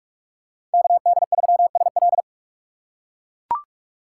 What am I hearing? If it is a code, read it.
Morse code, heard as a keyed tone, about 42 wpm, 700 Hz. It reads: KD3SL
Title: USA Call Signs - 5 Characters - 6x Character Spacing - 45wpm